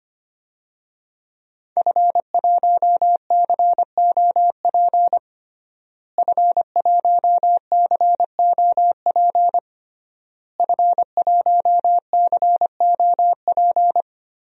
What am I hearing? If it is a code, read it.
Morse code, heard as a keyed tone, about 25 wpm, 700 Hz. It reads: F1COP F1COP F1COP
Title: DXCC-100 Call Signs - 5 Characters - Repeated 3x - 25wpm